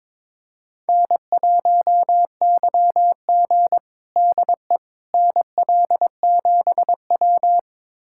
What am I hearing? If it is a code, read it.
Morse code, heard as a keyed tone, about 22 wpm, 700 Hz. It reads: N1YG DE NL7W
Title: USA Callsign DE Callsign - 4 Characters - 22wpm